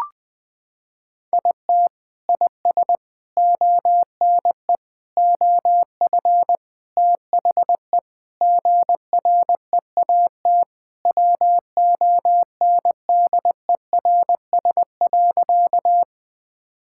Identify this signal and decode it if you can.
Morse code, heard as a keyed tone, about 21 wpm, 700 Hz. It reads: IT IS ONE OF THE GREAT WONDERS.
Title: Sentences with 4-Word Phrases - 20wpm